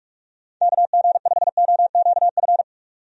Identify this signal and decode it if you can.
Morse code, heard as a keyed tone, about 46 wpm, 700 Hz. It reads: KG5OQF
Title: USA Call Signs - 6 Characters - 45wpm